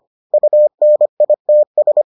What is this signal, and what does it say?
Morse code, heard as a keyed tone, about 25 wpm, 600 Hz. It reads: UNITS